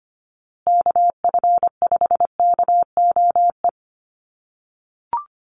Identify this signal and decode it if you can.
Morse code, heard as a keyed tone, about 25 wpm, 700 Hz. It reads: KF5KOE
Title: USA Call Signs Repeated 3x 25wpm